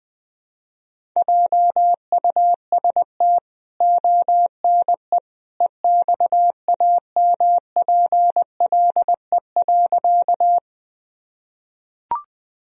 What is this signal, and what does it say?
Morse code, heard as a keyed tone, about 20 wpm, 700 Hz. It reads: JUST ONE EXAMPLE.
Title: Sentences from Top 1000 Words - Review Part 1 - 20wpm